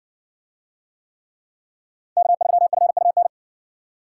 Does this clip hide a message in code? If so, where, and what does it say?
Yes: Morse code from 2.2 to 3.3 s, heard as a keyed tone, about 60 wpm, 700 Hz. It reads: K3FUN